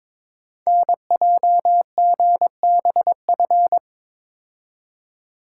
Morse code, heard as a keyed tone, about 23 wpm, 700 Hz. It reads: NJGBF
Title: Code Groups - 5 Letters - 22wpm